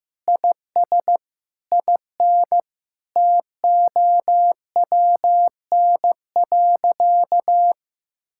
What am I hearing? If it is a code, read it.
Morse code, heard as a keyed tone, about 15 wpm, 700 Hz. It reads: IS IN TOWN.
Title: Sentences from Top 1000 Words - Review Part 5 - 15wpm